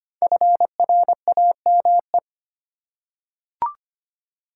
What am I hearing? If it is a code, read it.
Morse code, heard as a keyed tone, about 25 wpm, 700 Hz. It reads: FRAME
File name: practice / 25wpm / Easy Words with Hints - 25wpm